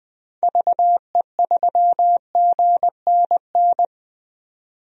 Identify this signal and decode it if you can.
Morse code, heard as a keyed tone, about 21 wpm, 700 Hz. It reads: VE3GNN
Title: DXCC-100 Call Signs - 6 Characters - 20wpm